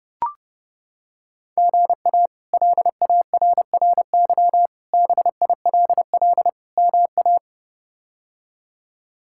Morse code, heard as a keyed tone, about 30 wpm, 700 Hz. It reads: GA LARRY BILL MA
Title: K1USN SST QSO Practice - 30wpm